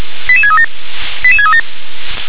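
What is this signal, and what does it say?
Five-tone selective call (ZVEI), heard as quick runs of five tones, tones of about 70 ms. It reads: 88637 88637
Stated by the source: Received on an SDR; ZVEI five-tone selective call, code 88915.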